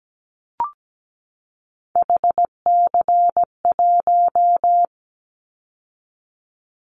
Morse code, heard as a keyed tone, about 17 wpm, 700 Hz. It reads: HC1
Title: Callsign Prefixes for Top 100 DXCC - 17wpm